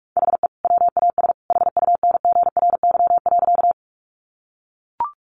Morse code, heard as a keyed tone, about 44 wpm, 700 Hz. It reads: HE WAS HUNGRY.
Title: Sentences from Top 1400 Words 45wpm